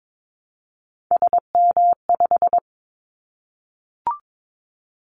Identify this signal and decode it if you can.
Morse code, heard as a keyed tone, about 22 wpm, 700 Hz. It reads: SM5
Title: Callsign Prefixes for Top 25 DXCC - 22wpm